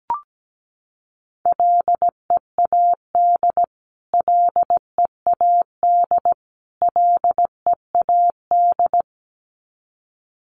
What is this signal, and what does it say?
Morse code, heard as a keyed tone, about 17 wpm, 700 Hz. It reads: LEAD LEAD LEAD
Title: Top 500 Words Repeated 17wpm